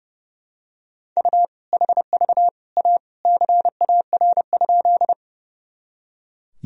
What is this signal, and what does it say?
Morse code, heard as a keyed tone, about 30 wpm, 700 Hz. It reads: U HV A CAR?
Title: Sentences from Top 300 Words and Abbreviations 30wpm